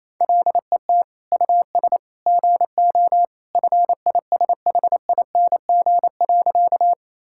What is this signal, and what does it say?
Morse code, heard as a keyed tone, about 28 wpm, 700 Hz. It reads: LET US GO FISHING.
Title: Sentences from Top 1000 Words - Review Part 5 - 28wpm